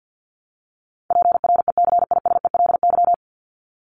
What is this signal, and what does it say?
Morse code, heard as a keyed tone, about 50 wpm, 700 Hz. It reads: PRECISELY
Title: LY Suffix - Top 100 Words with Base Words 50wpm